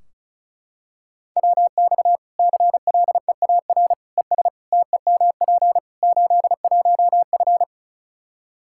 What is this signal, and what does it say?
Morse code, heard as a keyed tone, about 36 wpm, 700 Hz. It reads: WX CLEAR ES TEMP 81F
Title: Practice QSO - Exchange 2 - Info - 35wpm